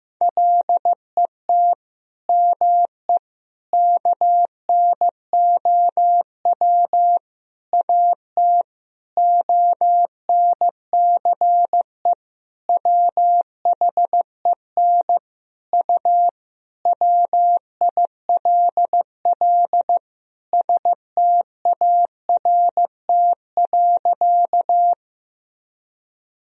Morse code, heard as a keyed tone, about 15 wpm, 700 Hz. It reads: LET ME KNOW AT ONCE WHEN U WILL START.
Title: Sentences from Top 300 Words and Abbreviations 15wpm